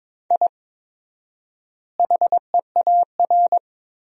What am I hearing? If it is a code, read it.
Morse code, heard as a keyed tone, about 22 wpm, 700 Hz. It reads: I HEAR
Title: 4x Word-Spacing - Sentences from Top 500 Words - 22wpm